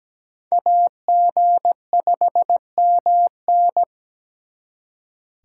Morse code, heard as a keyed tone, about 20 wpm, 700 Hz. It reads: AG5MN
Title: Speed Racing - USA 2x2 Call Signs 17wpm